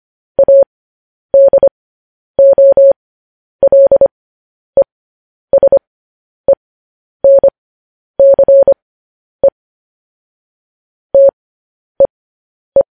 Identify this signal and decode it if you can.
Morse code, heard as a keyed tone, about 25 wpm, spaced as for 10 wpm, 550 Hz. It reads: ADOLESENCE TEE
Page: news-headlines-in-morse-code-at-10-wpm